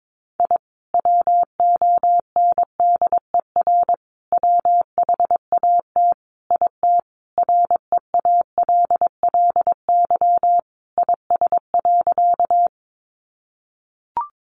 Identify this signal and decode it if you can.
Morse code, heard as a keyed tone, about 22 wpm, 700 Hz. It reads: I WONDER WHAT IT REALLY IS.
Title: Sentences from Top 600 Words 22wpm